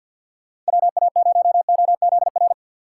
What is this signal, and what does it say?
Morse code, heard as a keyed tone, about 50 wpm, 700 Hz. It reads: WA0OZR
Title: USA Call Signs - 6 Characters - 4x Character Spacing - 50wpm